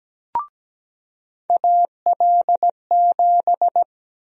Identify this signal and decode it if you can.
Morse code, heard as a keyed tone, about 17 wpm, 700 Hz. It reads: AL7
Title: USA Call Sign Prefixes - 17wpm